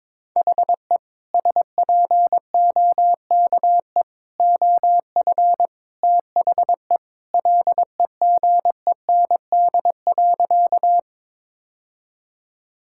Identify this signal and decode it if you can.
Morse code, heard as a keyed tone, about 22 wpm, 700 Hz. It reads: HE SPOKE OF THE LEGEND.
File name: practice / 22wpm / Sentences with 2-Word Phrases - 22wpm